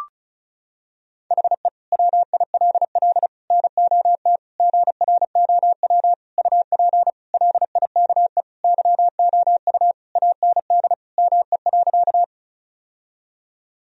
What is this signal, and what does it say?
Morse code, heard as a keyed tone, about 36 wpm, 700 Hz. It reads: HE WILL NOT GROW UP LIKE YOU AND ME.
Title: Sentences from Top 1000 Words - Review Part 1 - 35wpm